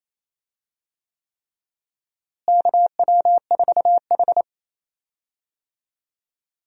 Morse code, heard as a keyed tone, about 28 wpm, 700 Hz. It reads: KW4H
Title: USA Call Signs - 4 Characters - Repeated 3x - 28wpm